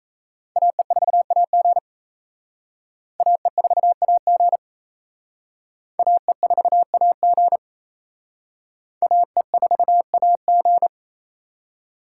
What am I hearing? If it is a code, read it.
Morse code, heard as a keyed tone, about 34 wpm, 700 Hz. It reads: AE4AG AE4AG AE4AG AE4AG
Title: Speed Racing - USA Call Signs 28wpm